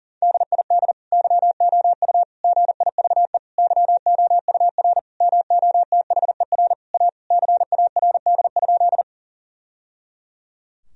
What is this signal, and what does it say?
Morse code, heard as a keyed tone, about 40 wpm, 700 Hz. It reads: DID YOU GIVE YOUR MOTHER A CARD?
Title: Sentences from Top 1000 Words - Review Part 4 - 40wpm